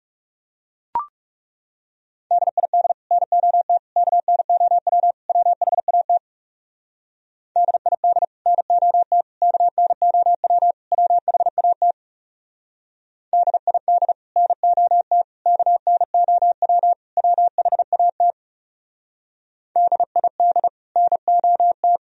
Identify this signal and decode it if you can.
Morse code, heard as a keyed tone, about 37 wpm, 700 Hz. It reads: DID NOT KNOW WHAT DID NOT KNOW WHAT DID NOT KNOW WHAT DID NOT
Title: Speed-Racing - Sets of 4 Words 30wpm